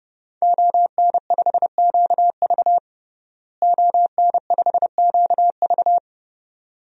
Morse code, heard as a keyed tone, about 30 wpm, 700 Hz. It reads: ON5QV ON5QV
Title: DXCC-100 Call Signs - 5 Characters - Repeated 2x - 30wpm